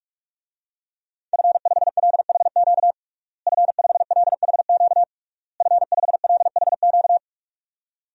Morse code, heard as a keyed tone, about 46 wpm, 700 Hz. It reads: F5LHQ F5LHQ F5LHQ
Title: DXCC-100 Call Signs - 5 Characters - Repeated 3x - 45wpm